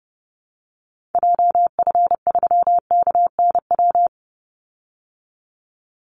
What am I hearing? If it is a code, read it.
Morse code, heard as a keyed tone, about 30 wpm, 700 Hz. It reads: JF3KNW
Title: Call Signs Repeated 3x - 30wpm